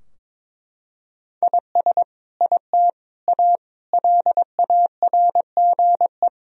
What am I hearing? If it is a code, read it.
Morse code, heard as a keyed tone, about 21 wpm, 700 Hz. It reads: IS IT A LARGE
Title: Sentences from Top 1000 Words - Review Part 4 - 22wpm